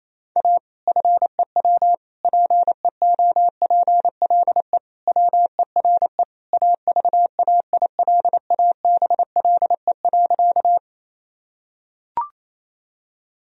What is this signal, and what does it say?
Morse code, heard as a keyed tone, about 28 wpm, 700 Hz. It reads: A FEW PEOPLE WERE AVAILABLE.
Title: Sentences from Top 1000 Words - Review Part 3 - 28wpm